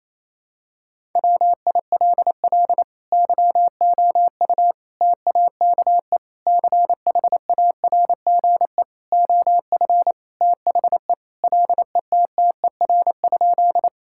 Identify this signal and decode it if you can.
Morse code, heard as a keyed tone, about 27 wpm, 700 Hz. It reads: WILL YOU TAKE CHARGE OF THE LETTER?
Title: Sentences from Top 1000 Words - Review Part 6 - 28wpm